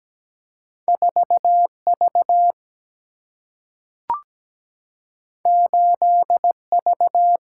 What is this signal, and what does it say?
Morse code, heard as a keyed tone, about 17 wpm, 700 Hz. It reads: 4V 8V
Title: Single-Letter Single-Number 17wpm